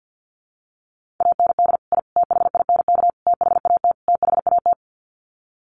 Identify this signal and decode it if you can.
Morse code, heard as a keyed tone, about 50 wpm, 700 Hz. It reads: AND I THINK THAT THAT
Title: Sets of 5 Words Encore 50wpm